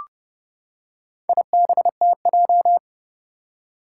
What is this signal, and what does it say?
Morse code, heard as a keyed tone, about 30 wpm, 700 Hz. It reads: IBTJ
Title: Four Challenging Letter-Number Combinations 30wpm